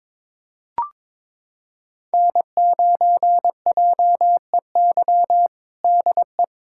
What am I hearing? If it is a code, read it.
Morse code, heard as a keyed tone, about 22 wpm, 700 Hz. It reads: N9JEY DE
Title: USA Callsign DE Callsign - 5 Characters - 22wpm